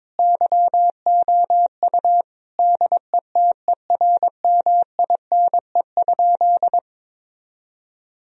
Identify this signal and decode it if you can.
Morse code, heard as a keyed tone, about 22 wpm, 700 Hz. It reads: YOU DETERMINE?
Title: Sentences from Top 1000 Words - Review Part 4 - 22wpm